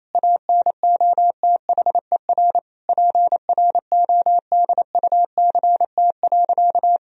Morse code, heard as a keyed tone, about 28 wpm, 700 Hz. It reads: ANOTHER PRODUCT.
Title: Sentences from Top 1000 Words - Review Part 3 - 28wpm